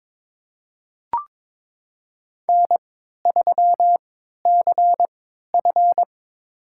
Morse code, heard as a keyed tone, about 22 wpm, spaced as for 13 wpm, 700 Hz. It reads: N3CF
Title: USA Call Signs - 4 Characters - 3x Character Spacing - 22wpm